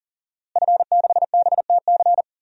Morse code, heard as a keyed tone, about 40 wpm, 700 Hz. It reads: F6BTC
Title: DXCC-100 Call Signs - 5 Characters - Repeated 2x - 40wpm